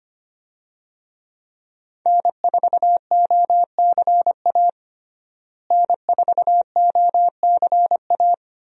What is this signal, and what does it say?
Morse code, heard as a keyed tone, about 25 wpm, 700 Hz. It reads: N4OCA N4OCA
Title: USA Call Signs - 5 Characters - Repeated 3x - 25wpm